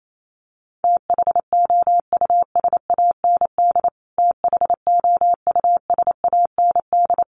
Morse code, heard as a keyed tone, about 28 wpm, 700 Hz. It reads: THOUSAND THOUSAND
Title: Top 500 Words Repeated 28wpm